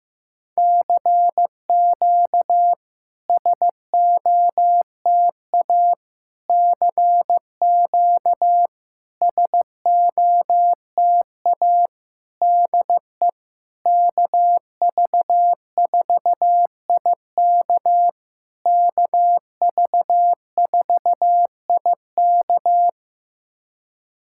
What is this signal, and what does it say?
Morse code, heard as a keyed tone, about 15 wpm, 700 Hz. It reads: CQ SOTA CQ SOTA DE KV4IK KV4IK
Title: SOTA QSO Practice 15wpm